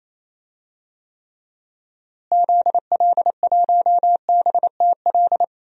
Morse code, heard as a keyed tone, about 28 wpm, 700 Hz. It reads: ZL1BTL
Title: DXCC-100 Call Signs - 6 Characters - Repeated 3x - 28wpm